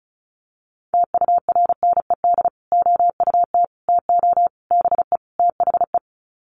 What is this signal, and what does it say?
Morse code, heard as a keyed tone, about 34 wpm, 700 Hz. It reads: TURNED OUT TO BE THE
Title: Sets of 5 Words Encore 35wpm